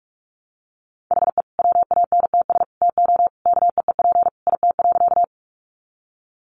Morse code, heard as a keyed tone, about 44 wpm, 700 Hz. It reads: HE WANTS TO KEEP IT.
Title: Additional Easy Phrases 45wpm